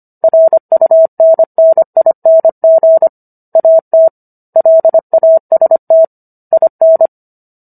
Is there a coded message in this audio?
Yes: Morse code from 0.2 to 7.1 s, heard as a keyed tone, about 25 wpm, 650 Hz. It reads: RUNNING AT LAST IN